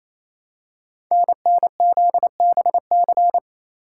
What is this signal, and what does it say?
Morse code, heard as a keyed tone, about 27 wpm, 700 Hz. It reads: NNZBC
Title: Code Groups - 5 Letters - 28wpm